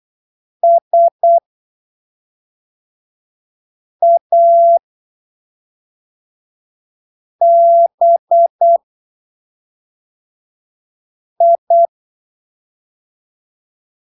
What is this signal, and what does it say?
Morse code, heard as a keyed tone, about 8 wpm, spaced as for 3 wpm, 680 Hz. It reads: SABI
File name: arquivos / texto / texto 16 08ppm